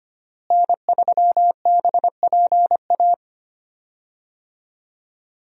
Morse code, heard as a keyed tone, about 25 wpm, 700 Hz. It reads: N3BPA
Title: USA Call Signs - 5 Characters - Repeated 2x - 25wpm